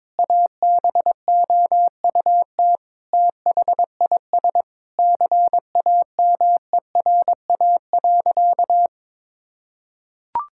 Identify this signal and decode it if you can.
Morse code, heard as a keyed tone, about 21 wpm, 700 Hz. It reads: ABOUT THIS CAMERA.
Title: Sentences from Top 1000 Words - Review Part 5 - 22wpm